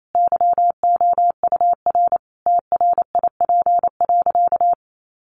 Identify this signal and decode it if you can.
Morse code, heard as a keyed tone, about 28 wpm, 700 Hz. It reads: YOUR TRIP.
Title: Sentences from Top 1500 Words 28wpm